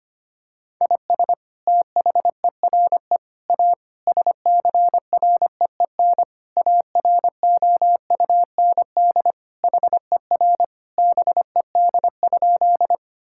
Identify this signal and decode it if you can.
Morse code, heard as a keyed tone, about 25 wpm, 700 Hz. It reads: IS THERE A SCREEN AROUND HER BED?
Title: Sentences from Top 1000 Words - Review Part 6 - 25wpm